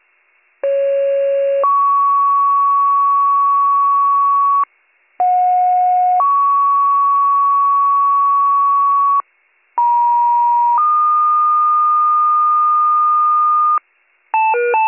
Two-tone page, heard as a one-second tone, then a three-second tone, about 550 then 1120 Hz, about 730 then 1120 Hz, about 950 then 1220 Hz. It is heard from 0.6 to 13.8 s.